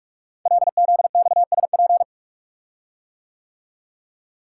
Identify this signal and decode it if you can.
Morse code, heard as a keyed tone, about 44 wpm, 700 Hz. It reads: LZXSP
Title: Code Groups - 5 Letters - 45wpm